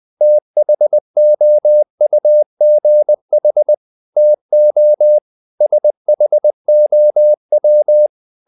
Morse code, heard as a keyed tone, about 20 wpm, 600 Hz. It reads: THOUGH TO SHOW
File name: the old man and the sea / the old man and the sea part 1 20wpm